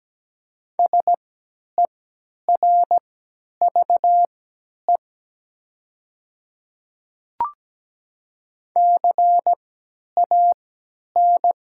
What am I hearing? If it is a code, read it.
Morse code, heard as a keyed tone, about 17 wpm, spaced as for 10 wpm, 700 Hz. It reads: SERVE CAN
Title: 3x Character Spacing - Top 500 Words - 17wpm